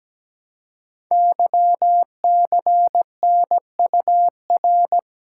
Morse code, heard as a keyed tone, about 17 wpm, 700 Hz. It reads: YCNUR
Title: Code Groups - 5 Letter:Numbers - 17wpm